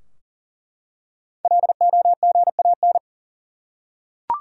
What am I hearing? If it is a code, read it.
Morse code, heard as a keyed tone, about 41 wpm, 700 Hz. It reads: LOGAN
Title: USA Common Names - 40wpm